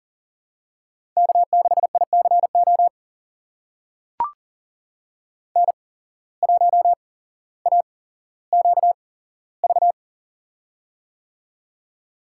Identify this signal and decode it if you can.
Morse code, heard as a keyed tone, about 40 wpm, spaced as for 17 wpm, 700 Hz. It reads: K6ICO N1AQV
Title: USA Call Signs - 5 Characters - 8x Character Spacing - 40wpm